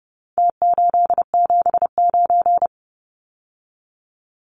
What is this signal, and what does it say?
Morse code, heard as a keyed tone, about 30 wpm, 700 Hz. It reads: T879
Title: Numbers - Contest Style - 4-Digit 30wpm